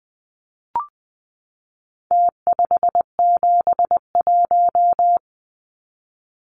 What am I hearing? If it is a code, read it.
Morse code, heard as a keyed tone, about 20 wpm, 700 Hz. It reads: T571
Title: Numbers - Contest Style - 4-Digit 20wpm